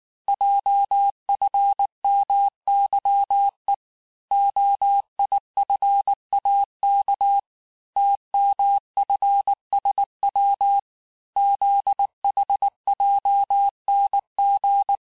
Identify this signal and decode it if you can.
Morse code, heard as a keyed tone, about 19 wpm, 800 Hz. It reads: JFMYE OIFAK TMFSW ZHJNG
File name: cours-du-lundi files / cours-1583--texte-chiffre-18-mpm